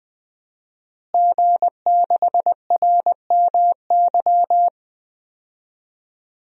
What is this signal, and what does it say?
Morse code, heard as a keyed tone, about 21 wpm, 700 Hz. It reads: G6RMY
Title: DXCC-100 Call Signs - 5 Characters - Repeated 2x - 20wpm